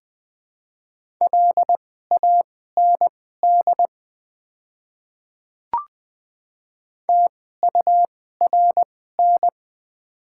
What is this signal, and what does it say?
Morse code, heard as a keyed tone, about 20 wpm, spaced as for 14 wpm, 700 Hz. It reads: LAND TURN
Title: 2x Character Spacing - Top 200 Words - 20wpm